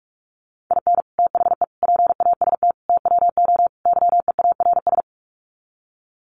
Morse code, heard as a keyed tone, about 44 wpm, 700 Hz. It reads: IN THE PAST TWO YEARS
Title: Sets of 5 Words 45wpm